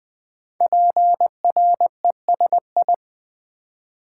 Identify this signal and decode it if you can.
Morse code, heard as a keyed tone, about 20 wpm, 700 Hz. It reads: PRESI
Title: Build A Word - 5 Letters At A Time - 12-Letter Words - 20wpm